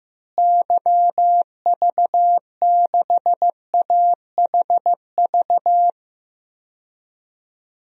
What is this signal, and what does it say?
Morse code, heard as a keyed tone, about 15 wpm, 700 Hz. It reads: YV6AHV
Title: DXCC-100 Call Signs - 6 Characters - Repeated 2x - 15wpm